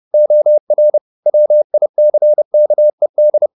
Morse code, heard as a keyed tone, about 30 wpm, 600 Hz. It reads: OR WICKED
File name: the old man and the sea / the old man and the sea 1 30wpm